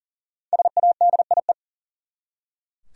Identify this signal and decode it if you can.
Morse code, heard as a keyed tone, about 40 wpm, 700 Hz. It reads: SADIE
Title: USA Common Names - 40wpm